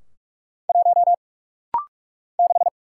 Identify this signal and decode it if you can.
Morse code, heard as a keyed tone, about 44 wpm, 700 Hz. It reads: 1 6
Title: Single Letter-Number - Rapid-Fire 45wpm